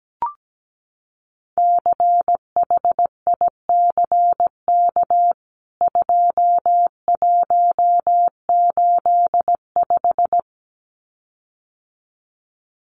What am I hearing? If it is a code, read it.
Morse code, heard as a keyed tone, about 17 wpm, 700 Hz. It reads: CHICK 2185
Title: CWT Exchange 17wpm